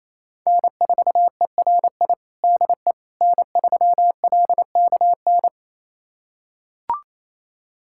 Morse code, heard as a keyed tone, about 27 wpm, 700 Hz. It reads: N4ERI DE N3LKN
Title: USA Callsign DE Callsign - 5 Characters - 28wpm